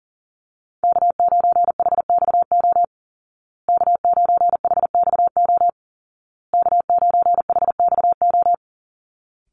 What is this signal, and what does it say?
Morse code, heard as a keyed tone, about 40 wpm, 700 Hz. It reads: K9HXO K9HXO K9HXO
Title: Call Signs Repeated 2x - 40wpm